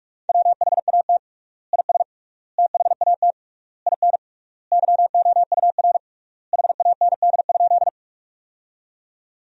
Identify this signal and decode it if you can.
Morse code, heard as a keyed tone, about 47 wpm, 700 Hz. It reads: WHAT IS THAT IN YOUR HAND?
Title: 3x Word-Spacing - Sentences from Top 200 Words - 45wpm